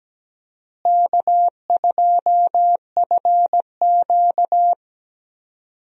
Morse code, heard as a keyed tone, about 17 wpm, 700 Hz. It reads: K2FQ
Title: USA Call Signs - 4 Characters - Repeated 3x - 17wpm